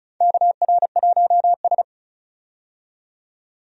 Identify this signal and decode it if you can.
Morse code, heard as a keyed tone, about 34 wpm, 700 Hz. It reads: KR1S
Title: USA Call Signs - 4 Characters - 35wpm